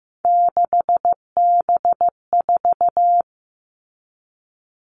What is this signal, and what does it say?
Morse code, heard as a keyed tone, about 15 wpm, 700 Hz. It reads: 6B4
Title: All Cut Numbers - 3-Digit 15wpm